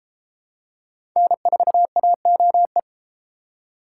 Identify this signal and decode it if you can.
Morse code, heard as a keyed tone, about 30 wpm, 700 Hz. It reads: N4AOE